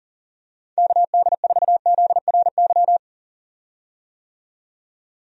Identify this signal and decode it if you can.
Morse code, heard as a keyed tone, about 40 wpm, 700 Hz. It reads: KD4ZRY
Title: USA Call Signs - 6 Characters - 40wpm